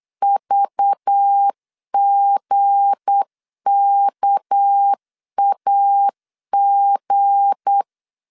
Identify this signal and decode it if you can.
Morse code, heard as a keyed tone, about 8 wpm, 800 Hz. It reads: VGKAG
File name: LEZIONI / 10 SIGNS / 10B INT 8wpm 3534